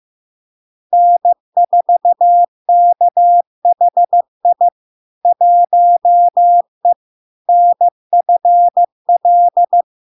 Morse code, heard as a keyed tone, about 15 wpm, 700 Hz. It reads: N4KHI 1E NFL
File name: LARK-FD-2022-Log-in-CW-15wpm-700-Hz